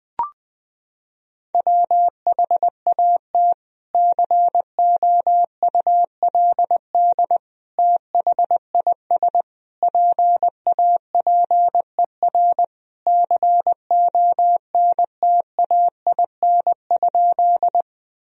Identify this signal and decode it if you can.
Morse code, heard as a keyed tone, about 21 wpm, 700 Hz. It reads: WHAT COULD THIS PAPER CONTAIN?
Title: Sentences from Top 1000 Words - Review Part 2 - 20wpm